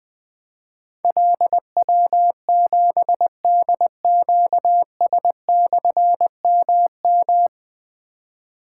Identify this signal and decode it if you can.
Morse code, heard as a keyed tone, about 20 wpm, 700 Hz. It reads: LW7DQS/MM
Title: Difficult Call Signs 20wpm